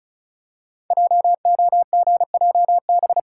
Morse code, heard as a keyed tone, about 36 wpm, 700 Hz. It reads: JOGJB
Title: Code Groups - 5 Letters - 35wpm